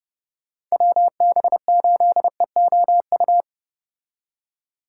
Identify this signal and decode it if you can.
Morse code, heard as a keyed tone, about 33 wpm, 700 Hz. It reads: WB8EOU